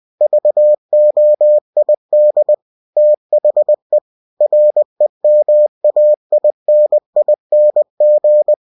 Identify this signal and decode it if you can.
Morse code, heard as a keyed tone, about 20 wpm, 600 Hz. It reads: VOID THE REMAINING
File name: cw-books / winnie-the-pooh / winnie-20wpm-page-0119